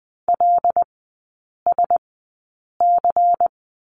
Morse code, heard as a keyed tone, about 20 wpm, spaced as for 8 wpm, 700 Hz. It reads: LSC
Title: 107 Lesson 18e - Letter m - Three Characters 20wpm